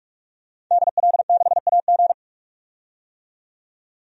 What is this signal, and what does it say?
Morse code, heard as a keyed tone, about 46 wpm, 700 Hz. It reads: DL6AG
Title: DXCC-100 Call Signs - 5 Characters - 45wpm